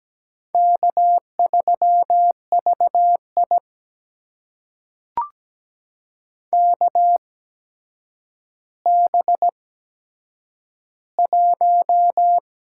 Morse code, heard as a keyed tone, about 17 wpm, spaced as for 8 wpm, 700 Hz. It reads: K3VI KB1
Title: USA Call Signs - 8x Character Spacing - 17wpm